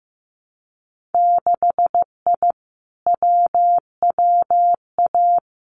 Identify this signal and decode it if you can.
Morse code, heard as a keyed tone, about 15 wpm, 700 Hz. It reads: 6I WWA
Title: Winter Field Day Exchanges Encore 15wpm